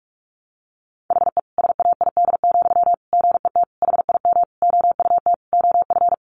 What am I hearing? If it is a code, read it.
Morse code, heard as a keyed tone, about 44 wpm, 700 Hz. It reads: HE SAID, GET HIM OUT OF
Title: Sentences from Top 400 Words 45wpm